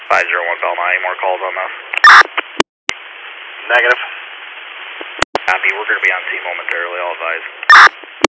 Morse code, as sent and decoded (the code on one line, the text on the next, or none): none